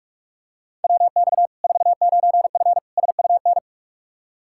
Morse code, heard as a keyed tone, about 44 wpm, 700 Hz. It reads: WX 49F SUN
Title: Practice QSO - Exchange 2 - Info - 45wpm